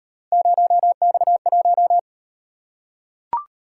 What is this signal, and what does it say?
Morse code, heard as a keyed tone, about 35 wpm, 700 Hz. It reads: 0X1